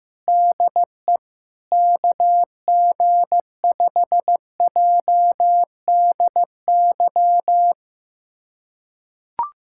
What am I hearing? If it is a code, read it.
Morse code, heard as a keyed tone, about 15 wpm, 700 Hz. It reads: DE KG5JDY
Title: USA Callsign DE Callsign 15wpm